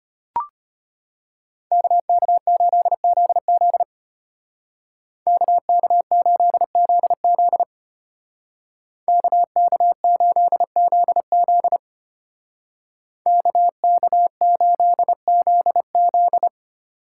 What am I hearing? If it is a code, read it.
Morse code, heard as a keyed tone, about 32 wpm, 700 Hz. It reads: KK8ZZ KK8ZZ KK8ZZ KK8ZZ